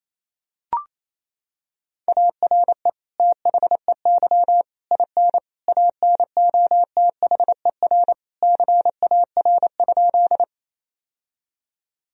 Morse code, heard as a keyed tone, about 28 wpm, 700 Hz. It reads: ARE THEY IN ANOTHER CAR?
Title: Sentences from Top 1000 Words - Review Part 1 - 28wpm